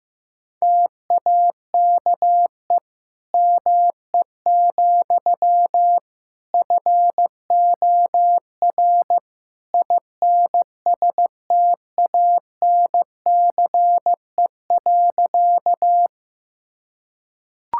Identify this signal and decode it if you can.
Morse code, heard as a keyed tone, about 15 wpm, 700 Hz. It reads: TAKE ME, FOR INSTANCE.
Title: Sentences from Top 1000 Words - Review Part 5 - 15wpm